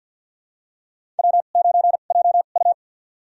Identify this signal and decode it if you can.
Morse code, heard as a keyed tone, about 49 wpm, spaced as for 36 wpm, 700 Hz. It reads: W9JU
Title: USA Call Signs - 2x Character Spacing - 50wpm